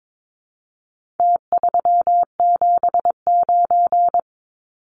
Morse code, heard as a keyed tone, about 22 wpm, 700 Hz. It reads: T379
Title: Numbers - Contest Style - 4-Digit 22wpm